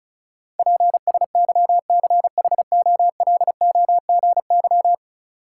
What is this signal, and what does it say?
Morse code, heard as a keyed tone, about 34 wpm, 700 Hz. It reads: PSYCHOLOGY
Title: Sentences from Top 1400 Words - New Word Drill - 35wpm